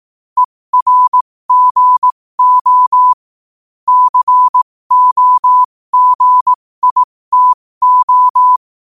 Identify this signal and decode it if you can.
Morse code, heard as a keyed tone, about 18 wpm, spaced as for 15 wpm, 1000 Hz. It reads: ERGO COGITO
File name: 15wpm